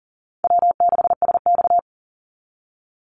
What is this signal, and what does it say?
Morse code, heard as a keyed tone, about 40 wpm, 700 Hz. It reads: W6SX
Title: Easy Call Signs Repeated 3x - 40wpm